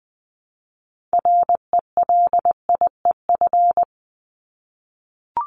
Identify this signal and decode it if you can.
Morse code, heard as a keyed tone, about 20 wpm, 700 Hz. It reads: RELIEF
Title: Top 1501-1750 Words 20wpm